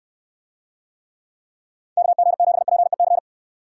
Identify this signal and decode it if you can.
Morse code, heard as a keyed tone, about 69 wpm, 700 Hz. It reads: KG7PEC